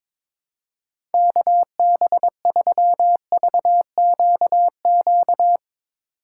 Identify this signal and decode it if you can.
Morse code, heard as a keyed tone, about 22 wpm, 700 Hz. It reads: KB3VQQ
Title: USA Call Signs - 6 Characters - 3x Character Spacing - 22wpm